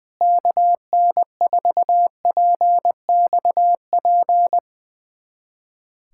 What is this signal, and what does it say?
Morse code, heard as a keyed tone, about 20 wpm, 700 Hz. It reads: KN4PXP
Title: USA Callsign DE Callsign 20wpm